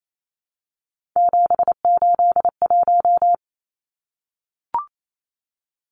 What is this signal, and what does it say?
Morse code, heard as a keyed tone, about 28 wpm, 700 Hz. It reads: 781
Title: Numbers - 3-Digit 28wpm